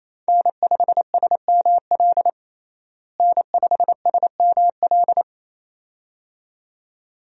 Morse code, heard as a keyed tone, about 28 wpm, 700 Hz. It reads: N5SML N5SML
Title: USA Call Signs - 5 Characters - Repeated 2x - 28wpm